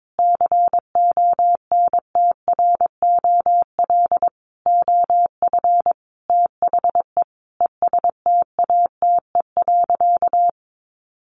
Morse code, heard as a keyed tone, about 22 wpm, 700 Hz. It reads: CONTROL OF THE ESTATE.
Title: Sentences from Top 1200 Words 22wpm